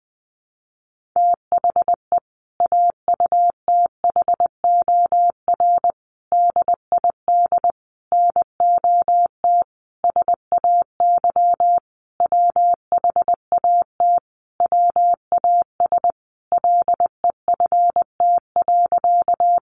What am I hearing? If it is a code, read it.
Morse code, heard as a keyed tone, about 20 wpm, 700 Hz. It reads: THE AUTHOR DID NOT SAY WHAT WAS LEFT.
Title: Sentences from Top 1200 Words 20wpm